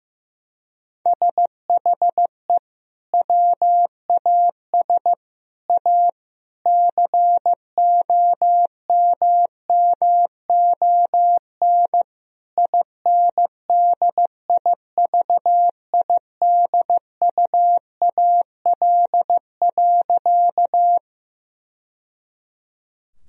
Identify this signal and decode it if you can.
Morse code, heard as a keyed tone, about 15 wpm, 700 Hz. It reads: SHE WAS A COMMON INDIVIDUAL.
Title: Sentences from Top 1000 Words - Review Part 4 - 15wpm